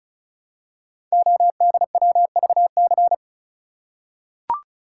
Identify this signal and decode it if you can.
Morse code, heard as a keyed tone, about 36 wpm, 700 Hz. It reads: ODWVC
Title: Code Groups - 5 Letters - 35wpm